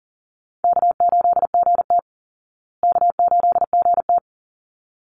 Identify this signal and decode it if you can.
Morse code, heard as a keyed tone, about 37 wpm, 700 Hz. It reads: K8GT K8GT
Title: Easy Call Signs Repeated 2x - 40wpm